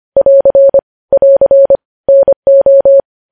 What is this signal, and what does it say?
Morse code, heard as a keyed tone, about 25 wpm, 550 Hz. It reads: <AR> <AR> NO
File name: podcast / News headlines-1x-25x25